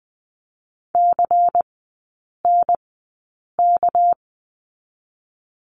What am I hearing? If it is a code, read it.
Morse code, heard as a keyed tone, about 20 wpm, spaced as for 8 wpm, 700 Hz. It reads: CNK
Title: 205 Lesson 33e - Letter j - Three Characters 20wpm